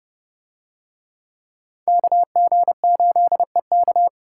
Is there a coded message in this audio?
Yes: Morse code from 1.9 to 4.1 s, heard as a keyed tone, about 28 wpm, 700 Hz. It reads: KG8EK